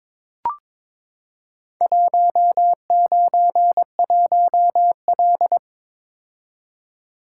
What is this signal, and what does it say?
Morse code, heard as a keyed tone, about 22 wpm, 700 Hz. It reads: 191L
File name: practice / 22wpm / Four Letter-Number Combinations - 22wpm